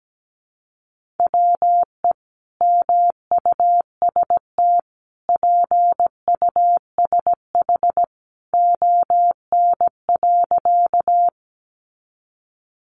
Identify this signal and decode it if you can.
Morse code, heard as a keyed tone, about 17 wpm, 700 Hz. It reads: WE MUST PUSH ON.
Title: Sentences from Top 1300 Words 17wpm